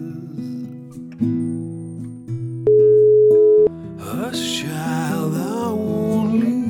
Ringback tone (Europe, 425 Hz).